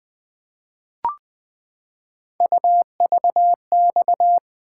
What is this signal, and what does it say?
Morse code, heard as a keyed tone, about 20 wpm, 700 Hz. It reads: UVX
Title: Three Letter-Number Combinations - 20wpm